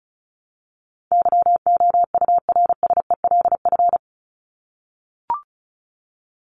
Morse code, heard as a keyed tone, about 34 wpm, 700 Hz. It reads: YOURSELF
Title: Top 501-750 Words 35wpm